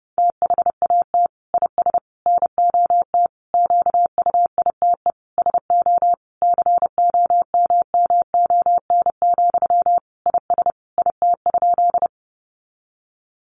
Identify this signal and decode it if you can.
Morse code, heard as a keyed tone, about 28 wpm, 700 Hz. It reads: THAT IS NOT QUITE SO COMMON, IS IT?
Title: Sentences from Top 700 Words 30wpm